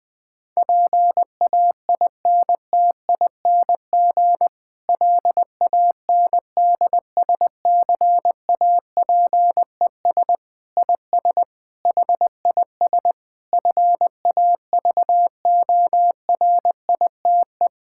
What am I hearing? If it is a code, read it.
Morse code, heard as a keyed tone, about 20 wpm, 700 Hz. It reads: PAINTING LANDSCAPES IS HIS FAVORITE
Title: Difficult Sentences with Hints - 20wpm